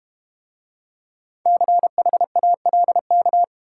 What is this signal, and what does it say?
Morse code, heard as a keyed tone, about 31 wpm, 700 Hz. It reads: CHALK
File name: Easy Words - Speed-Racing with Hints - 25wpm